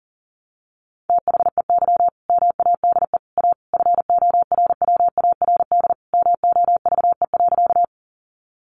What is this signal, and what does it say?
Morse code, heard as a keyed tone, about 40 wpm, 700 Hz. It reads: THEY MADE A FORWARD MOVE.
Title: Sentences from Top 1300 Words 40wpm